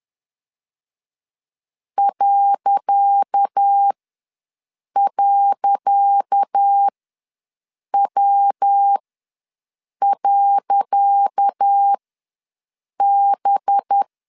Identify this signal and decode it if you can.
Morse code, heard as a keyed tone, about 10 wpm, spaced as for 6 wpm, 800 Hz. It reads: ..W.B
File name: LEZIONI / 10 SIGNS / 10C DOT ws 10wpm 4213